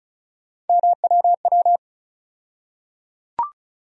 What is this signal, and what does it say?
Morse code, heard as a keyed tone, about 36 wpm, 700 Hz. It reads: MWW
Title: USA Call Sign Postfixes - 3 Letters 35wpm